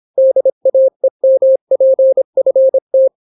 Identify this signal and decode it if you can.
Morse code, heard as a keyed tone, about 26 wpm, 530 Hz. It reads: DAEMPFT